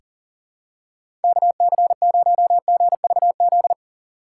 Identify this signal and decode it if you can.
Morse code, heard as a keyed tone, about 40 wpm, 700 Hz. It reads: KC0GVZ
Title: USA Call Signs - 6 Characters - 6x Character Spacing - 40wpm